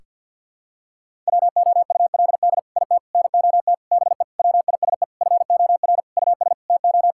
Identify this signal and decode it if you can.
Morse code, heard as a keyed tone, about 50 wpm, 700 Hz. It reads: WOULD IT NOT BE WISE FOR US TO
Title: Sentences from Top 1600 Words 50wpm